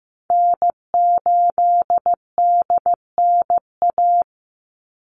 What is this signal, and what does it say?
Morse code, heard as a keyed tone, about 15 wpm, 700 Hz. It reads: N8DNA
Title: Call Signs Repeated 3x - 15wpm